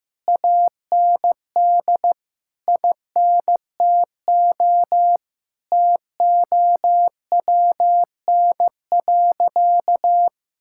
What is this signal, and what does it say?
Morse code, heard as a keyed tone, about 16 wpm, 700 Hz. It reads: AND INTO TOWN.
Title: Sentences from Top 1600 Words 15wpm